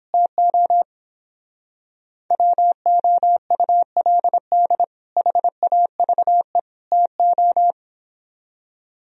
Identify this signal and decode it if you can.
Morse code, heard as a keyed tone, about 29 wpm, 700 Hz. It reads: TO WOULD HAVE TO